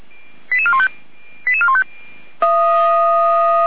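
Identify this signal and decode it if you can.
Five-tone selective call (ZVEI), heard as quick runs of five tones, tones of about 70 ms. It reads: 88415 88415
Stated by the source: ZVEI five-tone selective call received by SDR.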